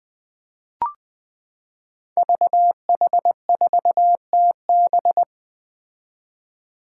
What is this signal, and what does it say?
Morse code, heard as a keyed tone, about 20 wpm, 700 Hz. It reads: VH4TB
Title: Five Challenging Letter-Number Combinations 20wpm